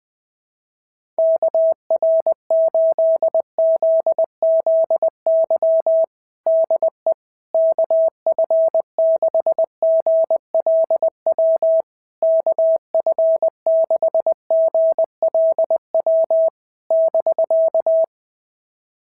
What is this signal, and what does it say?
Morse code, heard as a keyed tone, about 20 wpm, 660 Hz. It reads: KR8ZZY DE KF6GLW KF6GLW <BK>
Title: SOTA QSO Practice 20wpm